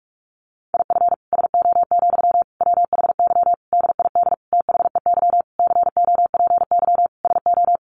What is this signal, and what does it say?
Morse code, heard as a keyed tone, about 44 wpm, 700 Hz. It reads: IF SO, WHY DID THEY COPY SO
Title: Sentences from Top 1400 Words 45wpm